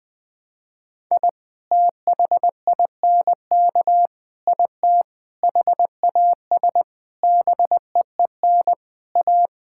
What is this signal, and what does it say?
Morse code, heard as a keyed tone, about 20 wpm, 700 Hz. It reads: I THINK IT HAS BEEN A
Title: Sentences from Top 1000 Words - Review Part 6 - 20wpm